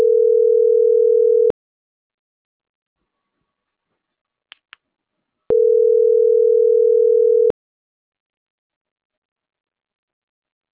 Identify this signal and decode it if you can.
Ringback tone (North America), 440 + 480 Hz.